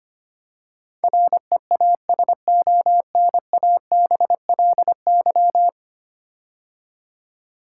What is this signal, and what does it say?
Morse code, heard as a keyed tone, about 25 wpm, 700 Hz. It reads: REASONABLY
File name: 10 Letter Words - First 3 Letters Given - 25wpm